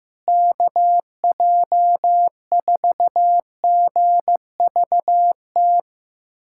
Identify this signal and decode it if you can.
Morse code, heard as a keyed tone, about 15 wpm, 700 Hz. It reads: KJ4GVT
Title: USA Callsign DE Callsign - 6 Characters - 15wpm